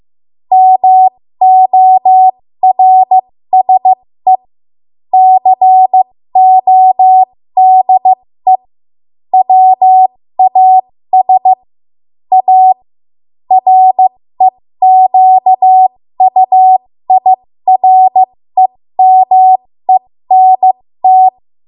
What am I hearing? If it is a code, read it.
Morse code, heard as a keyed tone, about 15 wpm, spaced as for 13 wpm, 750 Hz. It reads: MORSE CODE WAS A REQUIREMENT
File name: Morse / 13WPM